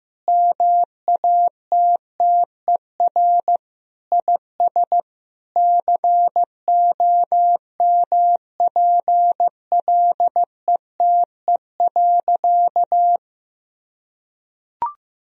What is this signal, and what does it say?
Morse code, heard as a keyed tone, about 15 wpm, 700 Hz. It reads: MATTER IS COMPLETE.
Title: Sentences from Top 1000 Words - Review Part 2 - 15wpm